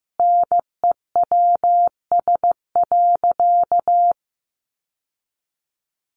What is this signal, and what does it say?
Morse code, heard as a keyed tone, about 15 wpm, 700 Hz. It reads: NEWS.
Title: Sentences from Top 1100 Words 15wpm